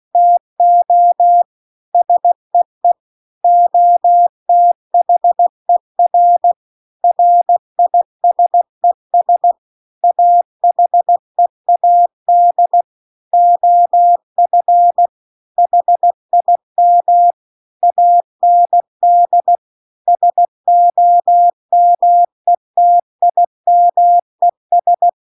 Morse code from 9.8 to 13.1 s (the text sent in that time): AHEAD